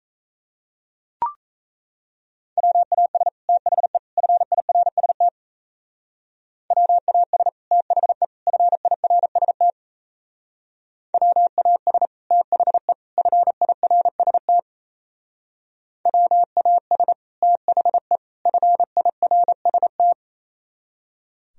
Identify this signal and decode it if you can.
Morse code, heard as a keyed tone, about 35 wpm, 700 Hz. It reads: WAS THE FIRST WAS THE FIRST WAS THE FIRST WAS THE FIRST
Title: Speed-Racing - Sets of 3 Words 28wpm